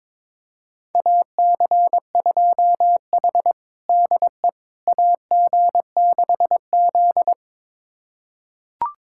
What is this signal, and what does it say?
Morse code, heard as a keyed tone, about 22 wpm, 700 Hz. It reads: AC2H DE AG6Z
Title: USA Callsign DE Callsign - 4 Characters - 22wpm